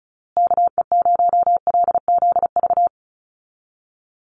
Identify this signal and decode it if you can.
Morse code, heard as a keyed tone, about 34 wpm, 700 Hz. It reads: KE0LZV
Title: USA Call Signs Repeated 2x 35wpm